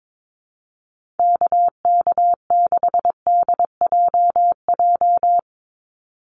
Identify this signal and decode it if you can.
Morse code, heard as a keyed tone, about 22 wpm, 700 Hz. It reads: KK6DJJ
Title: USA Call Signs 22wpm